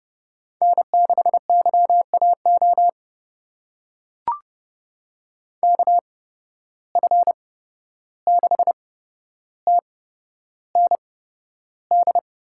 Morse code, heard as a keyed tone, about 30 wpm, spaced as for 12 wpm, 700 Hz. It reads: N6YAO KF6TND
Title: USA Call Signs - 8x Character Spacing - 30wpm